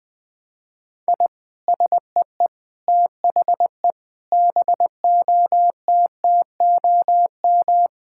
Morse code, heard as a keyed tone, about 21 wpm, 700 Hz. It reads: I SEE THE BOTTOM
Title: Sentences from Top 1000 Words - Review Part 6 - 20wpm